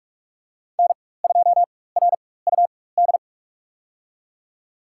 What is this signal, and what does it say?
Morse code, heard as a keyed tone, about 46 wpm, spaced as for 21 wpm, 700 Hz. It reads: N2RUD
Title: USA Call Signs - 5 Characters - 4x Character Spacing - 45wpm